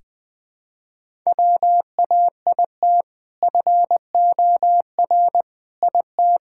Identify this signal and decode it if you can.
Morse code, heard as a keyed tone, about 20 wpm, 700 Hz. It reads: WAIT FOR IT
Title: Sentences with 2-Word Phrases - 20wpm